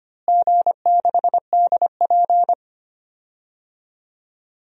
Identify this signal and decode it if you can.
Morse code, heard as a keyed tone, about 25 wpm, 700 Hz. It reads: G6DP
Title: DXCC-100 Call Signs - 4 Characters - Repeated 2x - 25wpm